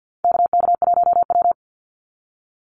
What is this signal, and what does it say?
Morse code, heard as a keyed tone, about 50 wpm, 700 Hz. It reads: KK1W
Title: NA SOTA Chaser Callsigns 50wpm